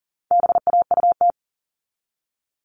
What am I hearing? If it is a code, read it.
Morse code, heard as a keyed tone, about 40 wpm, 700 Hz. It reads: BAUT
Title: All Cut Numbers - 4-Digit 40wpm